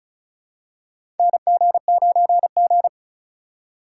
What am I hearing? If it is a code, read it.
Morse code, heard as a keyed tone, about 34 wpm, 700 Hz. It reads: NG9G
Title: USA Call Signs - 2x Character Spacing - 35wpm